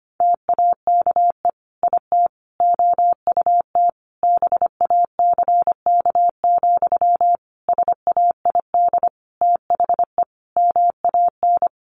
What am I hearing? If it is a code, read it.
Morse code, heard as a keyed tone, about 25 wpm, 700 Hz. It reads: TAKE IT OUT BACK, SAID THE MAN
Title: Sentences from Top 1500 Words 25wpm